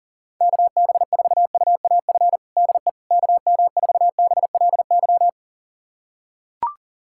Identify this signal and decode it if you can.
Morse code, heard as a keyed tone, about 37 wpm, 700 Hz. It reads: KB4UAF DE KM4BLY
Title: USA Callsign DE Callsign 40wpm